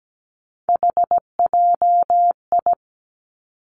Morse code, heard as a keyed tone, about 17 wpm, 700 Hz. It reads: HJI
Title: Three Challenging Letter-Number Combinations 17wpm